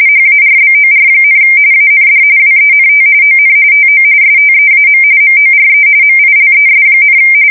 RTTY, about 45.45 baud. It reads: HARBOR, AND WENT ON TO STEAM MORE MILES,
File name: fsk170 / battleshipnj